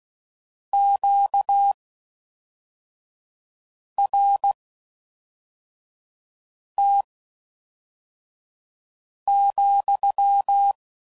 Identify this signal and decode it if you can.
Morse code, heard as a keyed tone, about 16 wpm, spaced as for 4 wpm, 800 Hz. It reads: QRT,